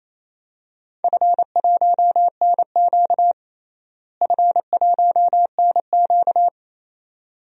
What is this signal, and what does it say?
Morse code, heard as a keyed tone, about 28 wpm, 700 Hz. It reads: F1NQ F1NQ
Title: DXCC-100 Call Signs - 4 Characters - Repeated 2x - 28wpm